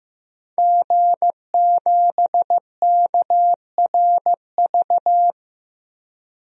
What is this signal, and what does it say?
Morse code, heard as a keyed tone, about 15 wpm, 700 Hz. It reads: G7KRV
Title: DXCC-100 Call Signs - 5 Characters - Repeated 2x - 15wpm